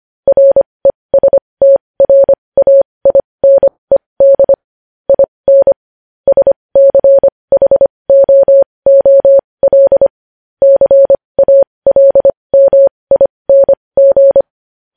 Morse code, heard as a keyed tone, about 25 wpm, spaced as for 20 wpm, 550 Hz. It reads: RESTRAINED IN SCHOOL CALMING
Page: news headlines in morse code at 20 wpm